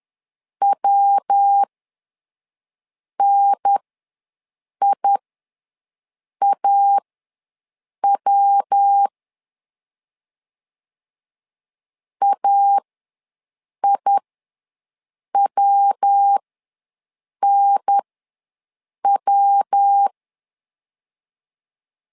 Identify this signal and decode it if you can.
Morse code, heard as a keyed tone, about 10 wpm, spaced as for 6 wpm, 800 Hz. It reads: WNIAW AIWNW
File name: LEZIONI / 3 MWRO / 3 MW ws 10wpm 2213